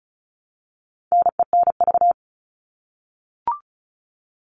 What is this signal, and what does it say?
Morse code, heard as a keyed tone, about 34 wpm, 700 Hz. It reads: NENV
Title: All Cut Numbers - 4-Digit 35wpm